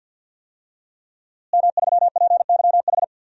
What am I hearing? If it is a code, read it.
Morse code, heard as a keyed tone, about 51 wpm, 700 Hz. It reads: M3PYH
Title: DXCC-100 Call Signs - 5 Characters - Repeated 3x - 50wpm